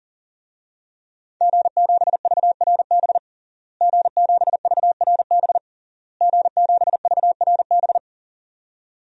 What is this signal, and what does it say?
Morse code, heard as a keyed tone, about 40 wpm, 700 Hz. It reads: G7VRB G7VRB G7VRB
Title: DXCC-100 Call Signs - 5 Characters - Repeated 3x - 40wpm